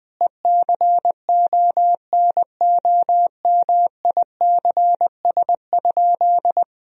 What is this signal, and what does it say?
Morse code, heard as a keyed tone, about 20 wpm, 700 Hz. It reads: ECONOMICS?
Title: Sentences from Top 1000 Words - Review Part 3 - 20wpm